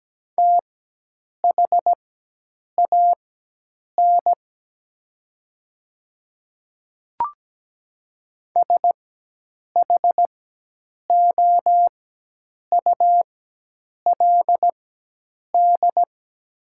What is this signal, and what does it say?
Morse code, heard as a keyed tone, about 17 wpm, spaced as for 8 wpm, 700 Hz. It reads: THAN SHOULD
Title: 4x Character Spacing - Top 500 Words - 17wpm